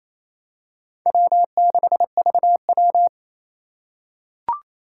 Morse code, heard as a keyed tone, about 28 wpm, 700 Hz. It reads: W6VW
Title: USA Call Signs - 4 Characters - 2x Character Spacing - 28wpm